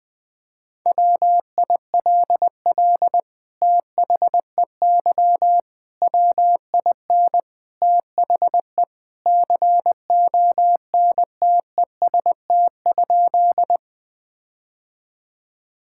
Morse code, heard as a keyed tone, about 20 wpm, 700 Hz. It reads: WILL THEY WIN THE CONTEST?
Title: Sentences from Top 1600 Words 20wpm